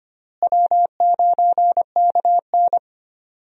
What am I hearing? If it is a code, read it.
Morse code, heard as a keyed tone, about 25 wpm, 700 Hz. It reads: W9KN
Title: USA Call Signs - 4 Characters - Repeated 2x - 25wpm